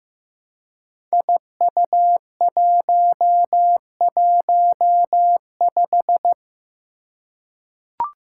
Morse code, heard as a keyed tone, about 15 wpm, 700 Hz. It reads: IU115
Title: Five Challenging Letter-Number Combinations 15wpm